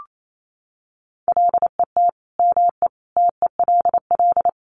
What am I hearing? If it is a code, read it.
Morse code, heard as a keyed tone, about 28 wpm, 700 Hz. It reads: LET ME TELL
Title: Sets of 5 Words Encore 28wpm